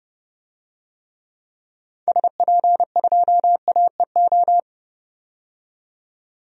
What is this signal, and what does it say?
Morse code, heard as a keyed tone, about 30 wpm, 700 Hz. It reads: SP2AEO
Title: DXCC-100 Call Signs - 6 Characters - 30wpm